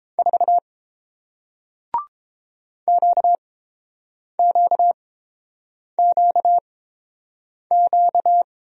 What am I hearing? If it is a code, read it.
Morse code, heard as a keyed tone, about 29 wpm, spaced as for 7 wpm, 700 Hz. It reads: 4 QQQQ